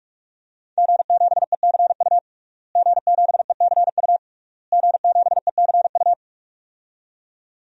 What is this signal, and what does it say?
Morse code, heard as a keyed tone, about 41 wpm, 700 Hz. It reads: G7ECU G7ECU G7ECU
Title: DXCC-100 Call Signs - 5 Characters - Repeated 3x - 45wpm